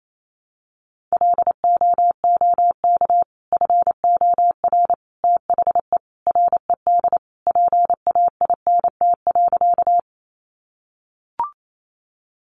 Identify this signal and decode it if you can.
Morse code, heard as a keyed tone, about 28 wpm, 700 Hz. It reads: LOOK FOR THE RED PAINT.
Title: A Sound Of Thunder 28wpm